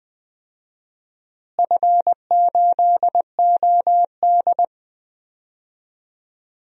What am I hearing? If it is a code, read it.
Morse code, heard as a keyed tone, about 20 wpm, 700 Hz. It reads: F8OD
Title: DXCC-100 Call Signs - 4 Characters - Repeated 2x - 20wpm